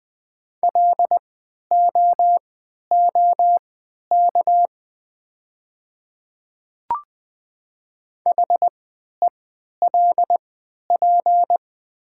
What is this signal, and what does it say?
Morse code, heard as a keyed tone, about 20 wpm, spaced as for 11 wpm, 700 Hz. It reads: LOOK HELP
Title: 3x Character Spacing - Top 200 Words - 20wpm